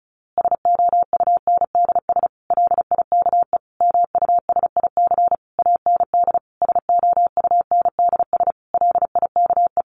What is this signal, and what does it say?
Morse code, heard as a keyed tone, about 34 wpm, 700 Hz. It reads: SOUNDS LIKE MUSIC AND SOUNDS LIKE
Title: A Sound Of Thunder 35wpm